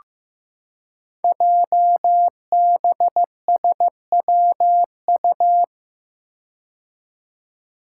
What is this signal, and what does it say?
Morse code, heard as a keyed tone, about 15 wpm, 700 Hz. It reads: JBSWU
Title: Code Groups - 5 Letter:Numbers - 15wpm